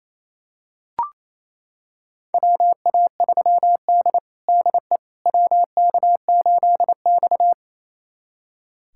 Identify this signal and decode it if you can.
Morse code, heard as a keyed tone, about 28 wpm, 700 Hz. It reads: WA3D DE WK8X
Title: USA Callsign DE Callsign - 4 Characters - 28wpm